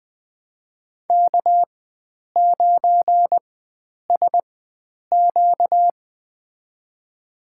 Morse code, heard as a keyed tone, about 21 wpm, spaced as for 9 wpm, 700 Hz. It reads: K9SQ
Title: USA Call Signs - 4 Characters - 4x Character Spacing - 20wpm